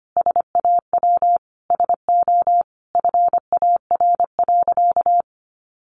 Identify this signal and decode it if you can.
Morse code, heard as a keyed tone, about 25 wpm, 700 Hz. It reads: SAW SO FAR.
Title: Sentences from Top 800 Words 25wpm